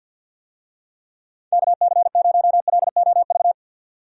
Morse code, heard as a keyed tone, about 51 wpm, 700 Hz. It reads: KC0LOV
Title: USA Call Signs - 6 Characters - 50wpm